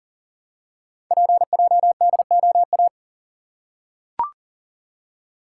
Morse code, heard as a keyed tone, about 38 wpm, 700 Hz. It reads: PJDOA
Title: Code Groups - 5 Letters - 40wpm